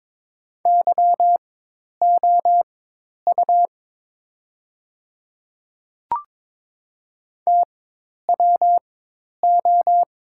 Morse code, heard as a keyed tone, about 22 wpm, spaced as for 10 wpm, 700 Hz. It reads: YOU TWO
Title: 4x Character Spacing - Top 100 Words - 22wpm